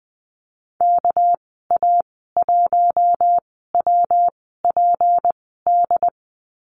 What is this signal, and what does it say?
Morse code, heard as a keyed tone, about 20 wpm, 700 Hz. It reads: K A 1 W P D
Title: 207 Lesson 33g - Letter j - Call Signs 20wpm